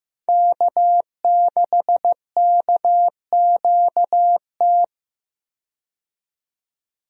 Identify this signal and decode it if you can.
Morse code, heard as a keyed tone, about 15 wpm, 700 Hz. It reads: K6KQT
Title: USA Call Signs - 5 Characters - 15wpm